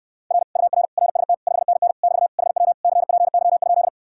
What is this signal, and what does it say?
Morse code, heard as a keyed tone, about 69 wpm, 700 Hz. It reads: R UR RST 5NN <BT> HW CPY?